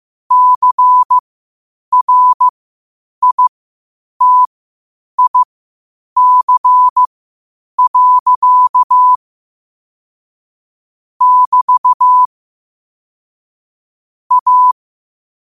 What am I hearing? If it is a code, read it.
Morse code, heard as a keyed tone, about 15 wpm, spaced as for 8 wpm, 1000 Hz. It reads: CRITIC. <BT> A
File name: qotdfiles / 8wpm